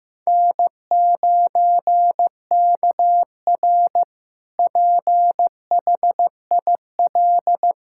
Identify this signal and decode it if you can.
Morse code, heard as a keyed tone, about 16 wpm, 700 Hz. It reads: N9KR PHIL
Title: K1USN SST QSO Practice - 15wpm